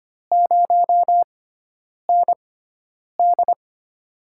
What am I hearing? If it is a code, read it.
Morse code, heard as a keyed tone, about 25 wpm, spaced as for 9 wpm, 700 Hz. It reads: 0ND
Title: USA Call Signs - 4 Characters - 6x Character Spacing - 25wpm